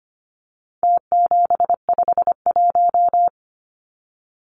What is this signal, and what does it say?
Morse code, heard as a keyed tone, about 25 wpm, 700 Hz. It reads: T751
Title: Numbers - Contest Style - 4-Digit 25wpm